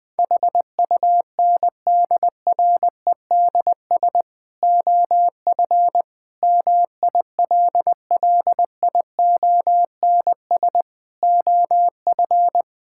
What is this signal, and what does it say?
Morse code, heard as a keyed tone, about 20 wpm, 700 Hz. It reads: HUNDREDS OF MILLIONS OF
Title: Sentences with 5-Word Phrases - 20wpm